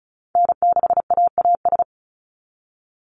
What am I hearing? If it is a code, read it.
Morse code, heard as a keyed tone, about 36 wpm, 700 Hz. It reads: N6AAS
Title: USA Call Signs 35wpm